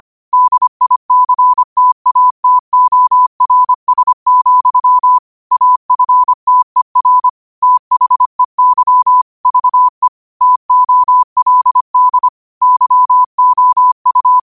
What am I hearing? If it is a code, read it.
Morse code, heard as a keyed tone, about 25 wpm, 1000 Hz. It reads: DICTATORS, AFTER THEY VE TOLD YOU